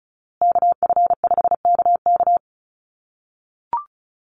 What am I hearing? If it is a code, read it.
Morse code, heard as a keyed tone, about 34 wpm, 700 Hz. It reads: KF5KK
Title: USA Call Signs Repeated 3x 35wpm